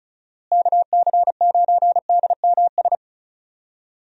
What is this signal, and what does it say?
Morse code, heard as a keyed tone, about 34 wpm, 700 Hz. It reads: KC9DMS
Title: USA Call Signs - 6 Characters - Repeated 3x - 35wpm